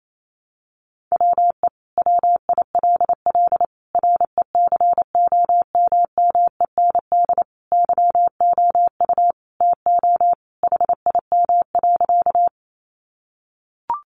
Morse code, heard as a keyed tone, about 28 wpm, 700 Hz. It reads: WE WILL RECOMMEND YOU TO HIM.
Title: Sentences from Top 1200 Words 28wpm